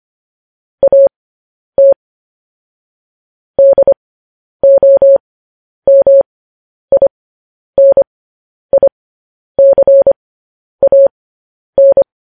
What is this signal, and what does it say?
Morse code, heard as a keyed tone, about 25 wpm, spaced as for 10 wpm, 550 Hz. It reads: AT DOMINICAN
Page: news headlines in morse code at 10 wpm